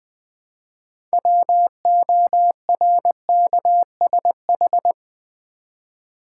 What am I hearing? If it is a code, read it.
Morse code, heard as a keyed tone, about 20 wpm, 700 Hz. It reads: WORKSH
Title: Build A Word - 3 Letters At A Time - 8-Letter Words - 20wpm